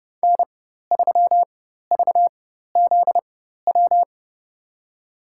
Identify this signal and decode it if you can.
Morse code, heard as a keyed tone, about 30 wpm, spaced as for 14 wpm, 700 Hz. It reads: N3VZW
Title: USA Call Signs - 5 Characters - 4x Character Spacing - 30wpm